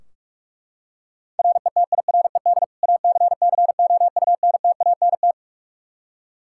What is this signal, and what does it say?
Morse code, heard as a keyed tone, about 42 wpm, 700 Hz. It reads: RETIRED ACCOUNTANT
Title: Practice QSO - Exchange 3 - More Info - 45wpm